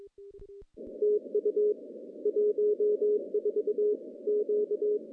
Morse code, heard as a keyed tone, about 23 wpm, 420 Hz. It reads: TU 14 Q
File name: contests / CW-SS-k5tr- -0100